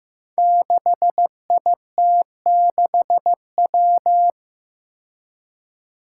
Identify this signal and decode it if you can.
Morse code, heard as a keyed tone, about 15 wpm, 700 Hz. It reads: 6IT6W
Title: Five Challenging Letter-Number Combinations 15wpm